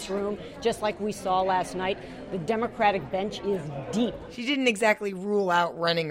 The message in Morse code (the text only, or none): none